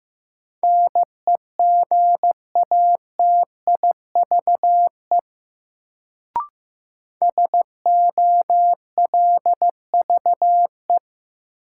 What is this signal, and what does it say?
Morse code, heard as a keyed tone, about 15 wpm, 700 Hz. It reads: NEGATIVE SOLVE
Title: Sentences from Top 1000 Words - New Word Drill - 15wpm